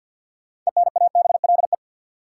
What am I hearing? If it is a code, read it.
Morse code, heard as a keyed tone, about 51 wpm, 700 Hz. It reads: ENABLE
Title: Sentences from Top 1500 Words - New Word Drill - 50wpm